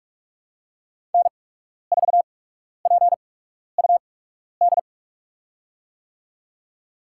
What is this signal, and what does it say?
Morse code, heard as a keyed tone, about 46 wpm, spaced as for 12 wpm, 690 Hz. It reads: N4PUD
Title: USA Call Signs - 8x Character Spacing - 45wpm